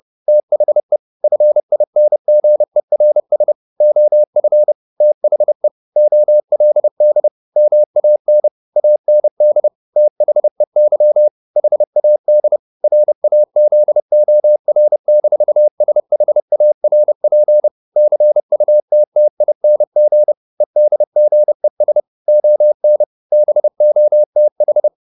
Morse code, heard as a keyed tone, about 30 wpm, 600 Hz. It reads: THE FINGERS OF THE OLD MAN AND THEY HAD RAZOR-SHARP CUTTING EDGES ON BOTH